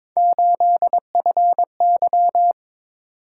Morse code, heard as a keyed tone, about 23 wpm, 700 Hz. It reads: 8FY
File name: practice / 22wpm / Three Letter-Number Combinations - 22wpm